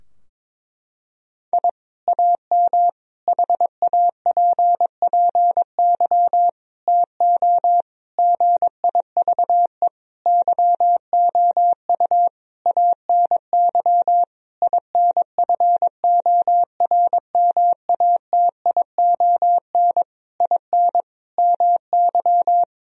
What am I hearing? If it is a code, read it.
Morse code, heard as a keyed tone, about 22 wpm, 700 Hz. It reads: I AM HAPPY TO GIVE YOU ANY INFORMATION IN MY
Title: Sentences from Top 1000 Words - Review Part 3 - 22wpm